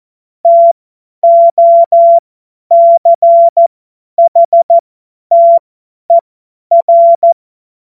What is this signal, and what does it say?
Morse code, heard as a keyed tone, about 14 wpm, spaced as for 10 wpm, 680 Hz. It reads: TOCHTER
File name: Morsekurs / Witze / Witz 10. Januar 2023 50 F70